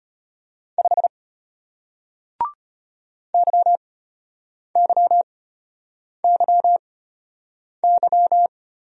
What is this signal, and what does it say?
Morse code, heard as a keyed tone, about 33 wpm, spaced as for 8 wpm, 690 Hz. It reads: 5 YYYY